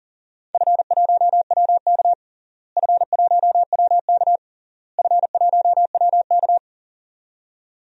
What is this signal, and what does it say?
Morse code, heard as a keyed tone, about 37 wpm, 700 Hz. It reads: F1WK F1WK F1WK
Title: DXCC-100 Call Signs - 4 Characters - Repeated 3x - 40wpm